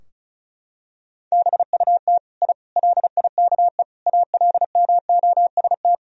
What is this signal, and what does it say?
Morse code, heard as a keyed tone, about 36 wpm, 700 Hz. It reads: BUT I LIKE ALMOST
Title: Sentences from Top 1000 Words - Review Part 4 - 35wpm